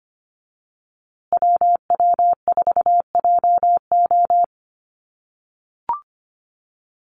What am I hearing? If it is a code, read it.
Morse code, heard as a keyed tone, about 25 wpm, 700 Hz. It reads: WW4JO
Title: USA Call Signs Repeated 2x 25wpm